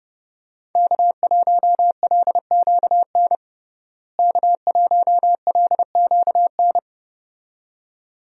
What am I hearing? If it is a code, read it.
Morse code, heard as a keyed tone, about 30 wpm, 700 Hz. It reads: K1LQN K1LQN
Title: USA Call Signs - 5 Characters - Repeated 2x - 30wpm